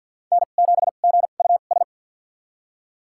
Morse code, heard as a keyed tone, about 47 wpm, spaced as for 35 wpm, 700 Hz. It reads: N7GUS
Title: USA Call Signs - 2x Character Spacing - 50wpm